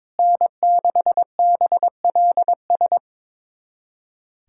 Morse code, heard as a keyed tone, about 22 wpm, 700 Hz. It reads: N6BLS
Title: USA Callsign DE Callsign - 5 Characters - 22wpm